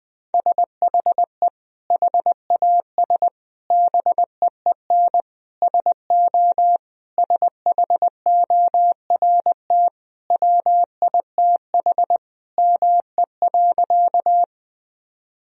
Morse code, heard as a keyed tone, about 20 wpm, 700 Hz. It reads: SHE HAS BEEN SO SHORT WITH ME.
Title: Sentences from Top 1000 Words - Review Part 2 - 20wpm